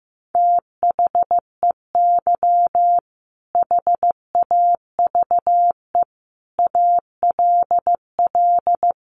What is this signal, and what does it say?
Morse code, heard as a keyed tone, about 15 wpm, 700 Hz. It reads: THEY HAVE ALL
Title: Sentences from Top 1100 Words 15wpm